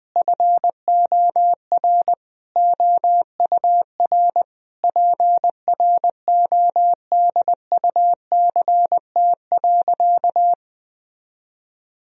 Morse code, heard as a keyed tone, about 20 wpm, 700 Hz. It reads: FOR OUR PRODUCT.
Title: Sentences from Top 1000 Words - Review Part 3 - 20wpm